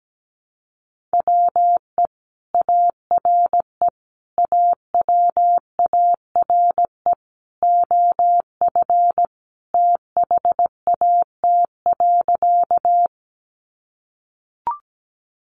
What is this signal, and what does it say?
Morse code, heard as a keyed tone, about 17 wpm, 700 Hz. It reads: WE ARE AWARE OF THAT.
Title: Sentences from Top 1200 Words 17wpm